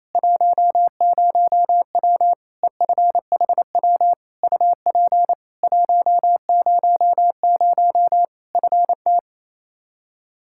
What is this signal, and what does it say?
Morse code, heard as a keyed tone, about 28 wpm, 700 Hz. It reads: 10W EFHW UP 100 FT
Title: K1USN SST QSO Practice 28wpm